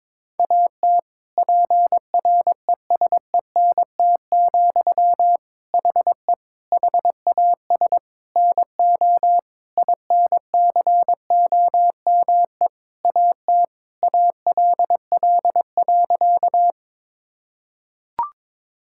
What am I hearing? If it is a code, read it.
Morse code, heard as a keyed tone, about 22 wpm, 700 Hz. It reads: AT PRESENT, HE HAS NO INCOME AT ALL.
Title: Sentences from Top 1000 Words - Review Part 5 - 22wpm